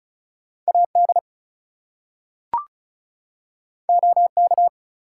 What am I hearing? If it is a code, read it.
Morse code, heard as a keyed tone, about 36 wpm, 700 Hz. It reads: AD OK
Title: Two Letter Words - 35wpm